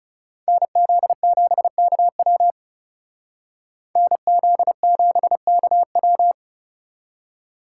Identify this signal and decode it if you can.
Morse code, heard as a keyed tone, about 38 wpm, 700 Hz. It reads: NZ7KW NZ7KW